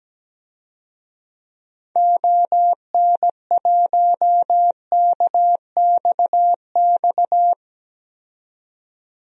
Morse code, heard as a keyed tone, about 17 wpm, 700 Hz. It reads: ON1KXX
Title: DXCC-100 Call Signs - 6 Characters - 17wpm